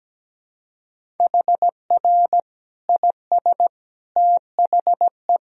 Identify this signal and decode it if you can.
Morse code, heard as a keyed tone, about 17 wpm, 700 Hz. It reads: HR IS THE
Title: Sentences from Top 300 Words and Abbreviations 17wpm